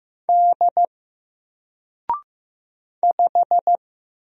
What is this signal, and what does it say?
Morse code, heard as a keyed tone, about 15 wpm, 700 Hz. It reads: D 5
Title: Single Letter-Number 15wpm